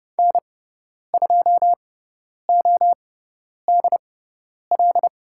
Morse code, heard as a keyed tone, about 30 wpm, spaced as for 10 wpm, 700 Hz. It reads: N2ODL
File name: Lesson 16g - Letter c - Call Signs 30wpm